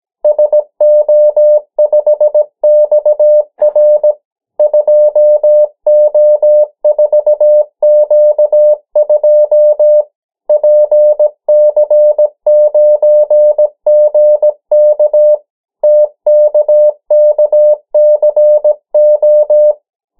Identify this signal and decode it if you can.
Morse code, heard as a keyed tone, about 17 wpm, 600 Hz. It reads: SO5XR 2O4Q2 PC9GK TKKCO
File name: morse mp3 / 14 urok 13